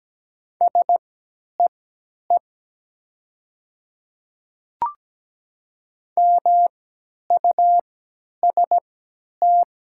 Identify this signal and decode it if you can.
Morse code, heard as a keyed tone, about 17 wpm, spaced as for 10 wpm, 700 Hz. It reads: SEE MUST
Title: 3x Character Spacing - Top 300 Words - 17wpm